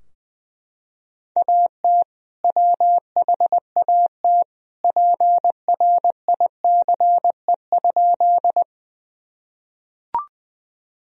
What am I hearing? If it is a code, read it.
Morse code, heard as a keyed tone, about 19 wpm, 700 Hz. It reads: AT WHAT PRICE?
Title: Sentences from Top 1000 Words - Review Part 4 - 20wpm